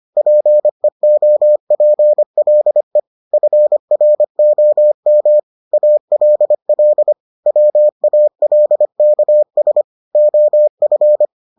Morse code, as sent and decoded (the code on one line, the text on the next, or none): .--. --- .--. .-.. / ..-. .-. --- -- / .- .-.. .-.. / .-- .- .-.. -.- ... / --- ..-.
PEOPLE FROM ALL WALKS OF